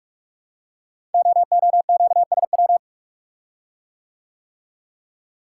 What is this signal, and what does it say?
Morse code, heard as a keyed tone, about 44 wpm, 700 Hz. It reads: OOQSW
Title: Code Groups - 5 Letters - 45wpm